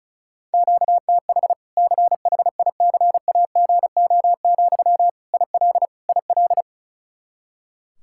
Morse code, heard as a keyed tone, about 34 wpm, 700 Hz. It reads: QTH CHICAGO, IL IL
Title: Practice QSO - Exchange 1 - Beginning - 35wpm